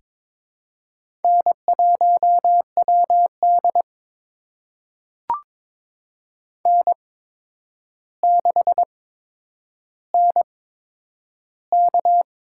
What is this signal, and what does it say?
Morse code, heard as a keyed tone, about 22 wpm, spaced as for 9 wpm, 700 Hz. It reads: N1WD N6NK
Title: USA Call Signs - 4 Characters - 8x Character Spacing - 22wpm